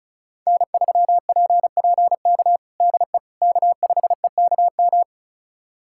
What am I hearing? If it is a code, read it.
Morse code, heard as a keyed tone, about 34 wpm, 700 Hz. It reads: N3PPK DE K5EKM
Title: USA Callsign DE Callsign - 5 Characters - 35wpm